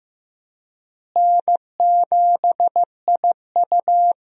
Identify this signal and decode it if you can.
Morse code, heard as a keyed tone, about 15 wpm, 700 Hz. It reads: N7IU
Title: USA Call Signs - 4 Characters - Repeated 3x - 15wpm